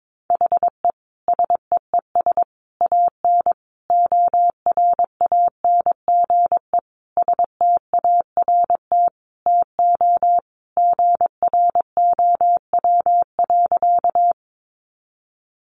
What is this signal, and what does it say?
Morse code, heard as a keyed tone, about 22 wpm, 700 Hz. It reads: HE SEES AN ORANGE START TO GROW.
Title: Easy Phrases 22wpm